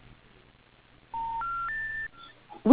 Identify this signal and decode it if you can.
Special information tone.